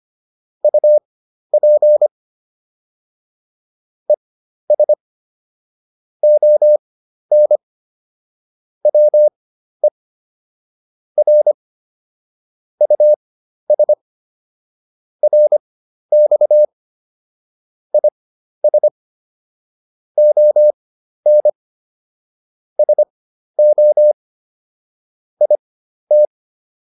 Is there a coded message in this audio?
Yes: Morse code from 0.6 to 26.3 s, heard as a keyed tone, about 25 wpm, spaced as for 12 wpm, 600 Hz. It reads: UP ES ON WE R US RX IS ON SO IT